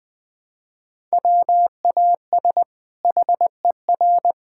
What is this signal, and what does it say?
Morse code, heard as a keyed tone, about 20 wpm, 700 Hz. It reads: WAS HER
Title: Sentences from Top 1000 Words - Review Part 5 - 20wpm